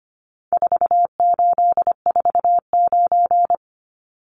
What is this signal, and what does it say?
Morse code, heard as a keyed tone, about 25 wpm, 700 Hz. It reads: 4849
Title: Numbers - 4-Digit 25wpm